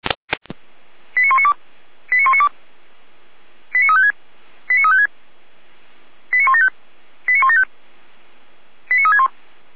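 Five-tone selective call (ZVEI), heard as quick runs of five tones, tones of about 70 ms. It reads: 89192 89192 89356 89356 89175 89175 89361
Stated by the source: ZVEI five-tone selective call received on an SDR.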